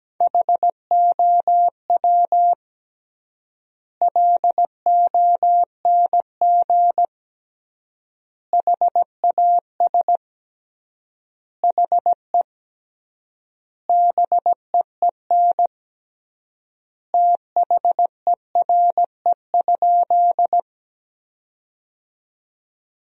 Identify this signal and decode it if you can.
Morse code, heard as a keyed tone, about 17 wpm, 700 Hz. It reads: HOW LONG HAS HE BEEN THERE?
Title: 3x Word-Spacing - Sentences from Top 100 Words - 17wpm